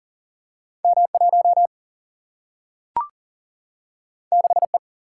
Morse code, heard as a keyed tone, about 41 wpm, 700 Hz. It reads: M1 6E
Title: Single-Letter Single-Number 40wpm